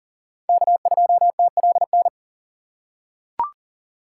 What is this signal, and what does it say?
Morse code, heard as a keyed tone, about 40 wpm, 700 Hz. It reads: K2TLN
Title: USA Call Signs - 5 Characters - 4x Character Spacing - 40wpm